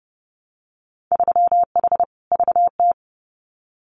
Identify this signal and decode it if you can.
Morse code, heard as a keyed tone, about 30 wpm, 700 Hz. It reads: 3H VT
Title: Winter Field Day Exchanges Encore 30wpm